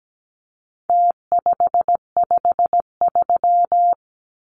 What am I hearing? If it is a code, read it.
Morse code, heard as a keyed tone, about 17 wpm, 700 Hz. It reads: T553
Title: Numbers - Contest Style - 4-Digit 17wpm